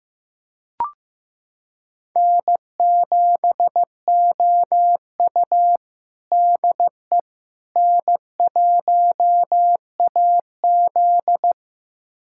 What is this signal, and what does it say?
Morse code, heard as a keyed tone, about 15 wpm, 700 Hz. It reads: N7OU DE N1AZ
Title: USA Callsign DE Callsign - 4 Characters - 15wpm